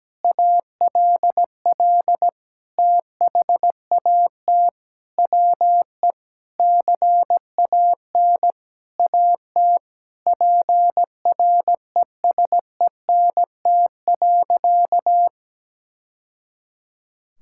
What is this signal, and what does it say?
Morse code, heard as a keyed tone, about 17 wpm, 700 Hz. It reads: ALL THAT WE CAN AT PRESENT.
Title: Sentences from Top 1000 Words - Review Part 5 - 17wpm